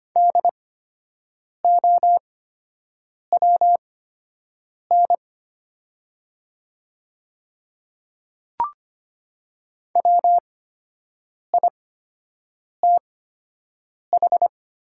Morse code, heard as a keyed tone, about 25 wpm, spaced as for 7 wpm, 700 Hz. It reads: DOWN WITH
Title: 8x Character Spacing - Top 100 Words - 25wpm